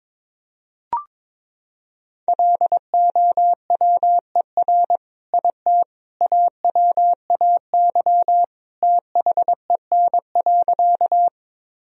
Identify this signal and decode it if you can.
Morse code, heard as a keyed tone, about 22 wpm, 700 Hz. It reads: LOWER IT AWAY THEN.
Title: Sentences from Top 1000 Words - Review Part 4 - 22wpm